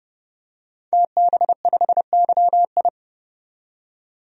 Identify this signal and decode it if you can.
Morse code, heard as a keyed tone, about 32 wpm, 700 Hz. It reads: TB5YI
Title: DXCC-100 Call Signs - 5 Characters - 30wpm